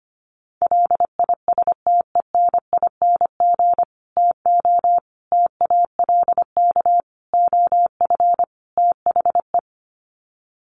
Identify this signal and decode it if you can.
Morse code, heard as a keyed tone, about 25 wpm, 700 Hz. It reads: LISTENING TO TALK OF THE
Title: Sets of 5 Words 25wpm